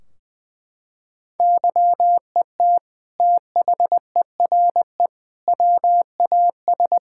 Morse code, heard as a keyed tone, about 20 wpm, 700 Hz. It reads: YET THERE WAS
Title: Sentences from Top 1600 Words 20wpm